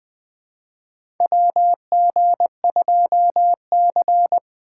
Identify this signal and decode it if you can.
Morse code, heard as a keyed tone, about 20 wpm, 700 Hz. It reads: WG2C
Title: USA Call Signs - 4 Characters - 2x Character Spacing - 20wpm